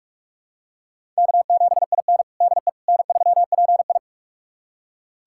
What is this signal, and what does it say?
Morse code, heard as a keyed tone, about 41 wpm, 700 Hz. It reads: K7IN DE N3PI
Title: USA Callsign DE Callsign - 4 Characters - 45wpm